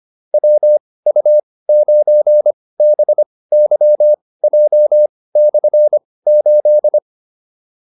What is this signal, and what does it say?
Morse code, heard as a keyed tone, about 25 wpm, spaced as for 18 wpm, 600 Hz. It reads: WU9BYJ/8